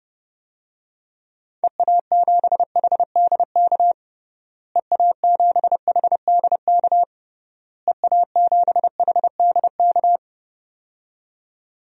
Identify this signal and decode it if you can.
Morse code, heard as a keyed tone, about 32 wpm, 700 Hz. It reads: EA7HDK EA7HDK EA7HDK
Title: DXCC-100 Call Signs - 6 Characters - Repeated 3x - 30wpm